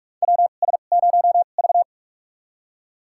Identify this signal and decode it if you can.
Morse code, heard as a keyed tone, about 46 wpm, spaced as for 32 wpm, 700 Hz. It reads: WS0V
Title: USA Call Signs - 4 Characters - 2x Character Spacing - 45wpm